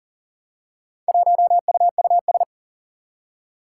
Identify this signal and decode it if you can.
Morse code, heard as a keyed tone, about 38 wpm, 700 Hz. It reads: 1UUS
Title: Four Challenging Letter-Number Combinations 40wpm